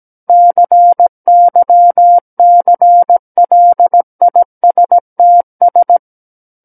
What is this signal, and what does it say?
Morse code, heard as a keyed tone, about 17 wpm, 700 Hz. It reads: CYCLISTS
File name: Book One / 08-Chapter 8 - 17 WPM